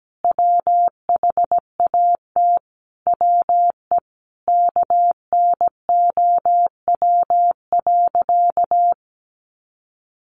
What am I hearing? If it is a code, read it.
Morse code, heard as a keyed tone, about 17 wpm, 700 Hz. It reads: WHAT WE KNOW.
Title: Sentences from Top 1200 Words 17wpm